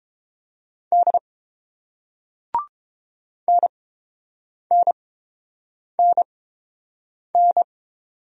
Morse code, heard as a keyed tone, about 30 wpm, spaced as for 7 wpm, 700 Hz. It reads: D NNNN